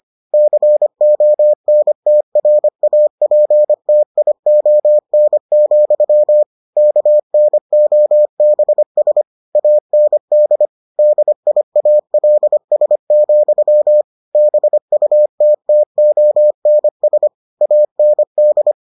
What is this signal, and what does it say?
Morse code, heard as a keyed tone, about 25 wpm, 600 Hz. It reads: CONTRAPTION, KNOBS AND DIALS, BUTTONS AND